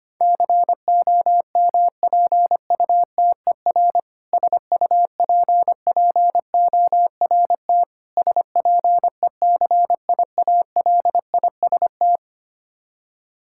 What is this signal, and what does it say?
Morse code, heard as a keyed tone, about 25 wpm, 700 Hz. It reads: COMPUTER SUPPORT SPECIALIST
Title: K1USN SST QSO Practice 25wpm